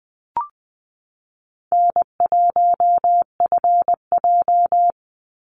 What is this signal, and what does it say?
Morse code, heard as a keyed tone, about 20 wpm, 700 Hz. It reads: N1FJ
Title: NA SOTA Activator Callsigns 20wpm